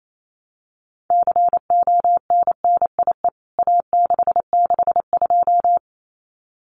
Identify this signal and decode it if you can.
Morse code, heard as a keyed tone, about 28 wpm, 700 Hz. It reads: CONNIE A662
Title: CWT Exchange 28wpm